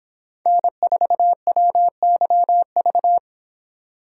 Morse code, heard as a keyed tone, about 25 wpm, 700 Hz. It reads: N4WYV